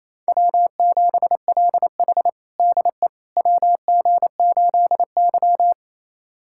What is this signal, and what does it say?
Morse code, heard as a keyed tone, about 28 wpm, 700 Hz. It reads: W7LH DE WG8Y
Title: USA Callsign DE Callsign - 4 Characters - 28wpm